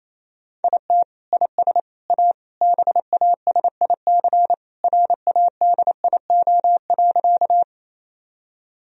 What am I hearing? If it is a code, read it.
Morse code, heard as a keyed tone, about 27 wpm, 700 Hz. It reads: IT IS A BASIC RADIO.
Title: Sentences from Top 1000 Words - Review Part 3 - 28wpm